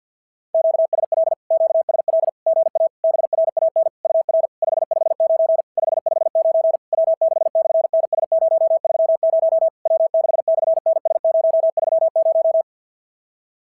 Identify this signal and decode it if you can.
Morse code, heard as a keyed tone, about 52 wpm, 690 Hz. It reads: QSL QSL GA BRAN UR 559 559 W6/NS030 W6/NS030
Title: SOTA QSO Practice 50wpm